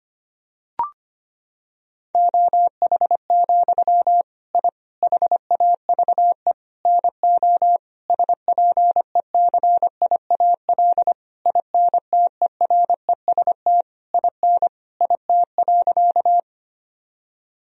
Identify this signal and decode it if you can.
Morse code, heard as a keyed tone, about 25 wpm, 700 Hz. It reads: OH, I HAVE NO SPECIAL INTEREST IN IT.
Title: Sentences from Top 1000 Words - Review Part 3 - 25wpm